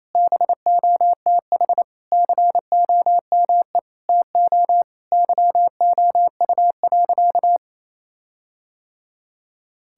Morse code, heard as a keyed tone, about 28 wpm, 700 Hz. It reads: BOTH COME TO YOU.
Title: Sentences from Top 1600 Words 28wpm